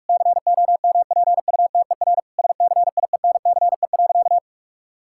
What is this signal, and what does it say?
Morse code, heard as a keyed tone, about 41 wpm, 700 Hz. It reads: COMPUTER SCIENCE.
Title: Sentences from Top 1000 Words - Review Part 6 - 45wpm